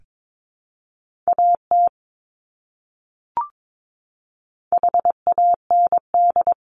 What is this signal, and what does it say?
Morse code, heard as a keyed tone, about 22 wpm, 700 Hz. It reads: AT HAND
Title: Top 300 Words Repeated 22wpm